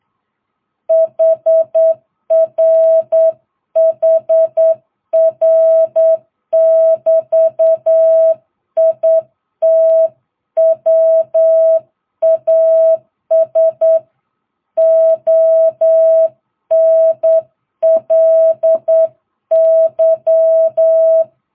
Morse code, heard as a keyed tone, about 9 wpm, 660 Hz. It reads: HR HR <BT> IT WAS ONLY